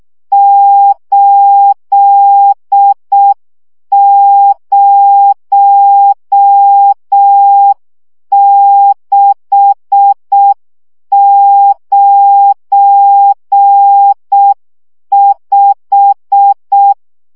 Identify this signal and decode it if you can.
Morse code, heard as a keyed tone, about 6 wpm, 800 Hz. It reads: 80695
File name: morze vizsga 2011-2 5